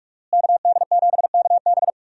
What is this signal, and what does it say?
Morse code, heard as a keyed tone, about 46 wpm, 700 Hz. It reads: KD7KB
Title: USA Call Signs - 5 Characters - Repeated 2x - 45wpm